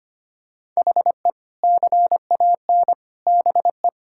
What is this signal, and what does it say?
Morse code, heard as a keyed tone, about 25 wpm, 700 Hz. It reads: HE CAN BE
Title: Sentences from Top 1600 Words 25wpm